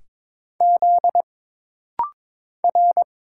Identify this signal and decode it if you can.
Morse code, heard as a keyed tone, about 22 wpm, 700 Hz. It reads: Z R
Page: This rapid-fire timing is geared to push your ICR (Instant Character Recognition) skills.